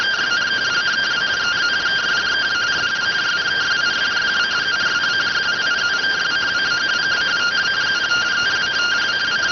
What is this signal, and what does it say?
Two-tone teleprinter signal, about 170 Hz shift.